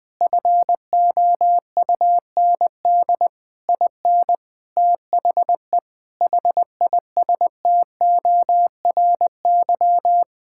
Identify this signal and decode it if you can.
Morse code, heard as a keyed tone, about 21 wpm, 700 Hz. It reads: FOUND IN THE HISTORY
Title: Sentences with 4-Word Phrases - 20wpm